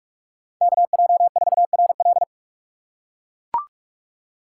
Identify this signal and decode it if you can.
Morse code, heard as a keyed tone, about 44 wpm, 700 Hz. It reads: KJ4RL